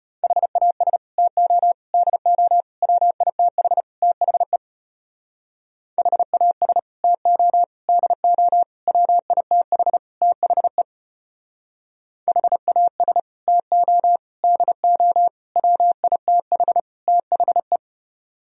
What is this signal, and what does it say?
Morse code, heard as a keyed tone, about 31 wpm, 700 Hz. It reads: HAS TO DO WITH THE HAS TO DO WITH THE HAS TO DO WITH THE
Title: Speed-Racing - Sets of 5 Words 25wpm